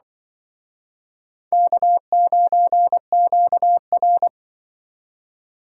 Morse code, heard as a keyed tone, about 25 wpm, 700 Hz. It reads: K9QR
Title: Speed Racing - USA 1x2 Call Signs 20wpm